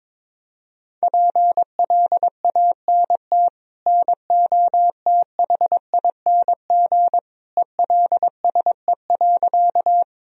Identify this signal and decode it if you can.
Morse code, heard as a keyed tone, about 22 wpm, 700 Hz. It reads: PLANT NOTHING ELSE.
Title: Sentences from Top 1000 Words - Review Part 1 - 22wpm